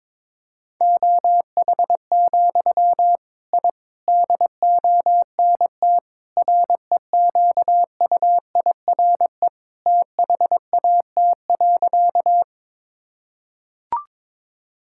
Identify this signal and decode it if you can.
Morse code, heard as a keyed tone, about 22 wpm, 700 Hz. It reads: OH, I DONT REQUIRE THAT.
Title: Sentences from Top 1000 Words - Review Part 6 - 22wpm